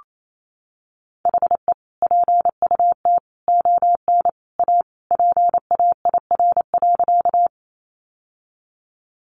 Morse code, heard as a keyed tone, about 28 wpm, 700 Hz. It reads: HE PUT ON A PAIR.
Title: Sentences from Top 1200 Words 28wpm